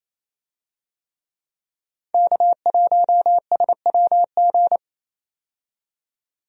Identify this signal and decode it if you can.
Morse code, heard as a keyed tone, about 28 wpm, 700 Hz. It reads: K1SWG
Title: USA Call Signs - 5 Characters - 28wpm